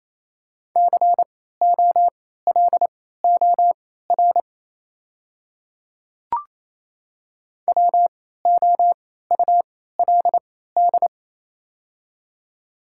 Morse code, heard as a keyed tone, about 28 wpm, spaced as for 16 wpm, 700 Hz. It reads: COLOR WOULD
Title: 3x Character Spacing - Top 500 Words - 28wpm